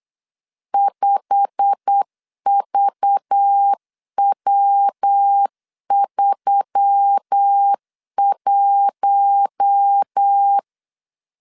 Morse code, heard as a keyed tone, about 8 wpm, 800 Hz. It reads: 5VW31
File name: CW / DET / 1 9 1500C S C 8wpm 5653